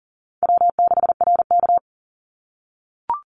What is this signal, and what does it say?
Morse code, heard as a keyed tone, about 41 wpm, 700 Hz. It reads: W6RK
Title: Easy Call Signs - 40wpm